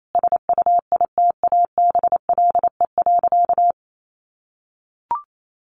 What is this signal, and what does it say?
Morse code, heard as a keyed tone, about 28 wpm, 700 Hz. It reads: SUITABLE.
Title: Sentences from Top 1400 Words 28wpm